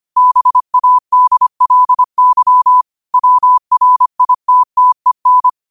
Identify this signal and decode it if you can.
Morse code, heard as a keyed tone, about 25 wpm, 1000 Hz. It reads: BADLY WRITTEN
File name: qotdfiles / 25wpm